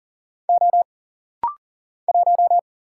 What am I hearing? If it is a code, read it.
Morse code, heard as a keyed tone, about 40 wpm, 700 Hz. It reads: O 1
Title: Single Letter-Number - Rapid-Fire 40wpm